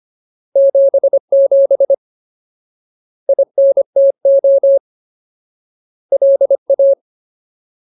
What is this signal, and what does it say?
Morse code, heard as a keyed tone, about 25 wpm, 550 Hz. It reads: 77 INTO LA
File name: Ultimate Word List / 25WPM Ultimate